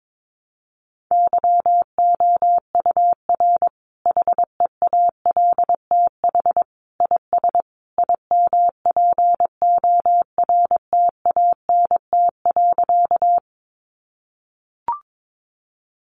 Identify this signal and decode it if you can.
Morse code, heard as a keyed tone, about 22 wpm, 700 Hz. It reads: YOUR HEALTH IS IMPORTANT.
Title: Sentences from Top 600 Words 22wpm